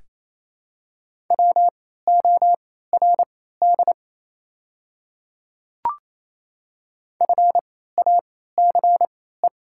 Morse code, heard as a keyed tone, about 27 wpm, spaced as for 16 wpm, 700 Hz. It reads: WORD FACE
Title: 3x Character Spacing - Top 300 Words - 28wpm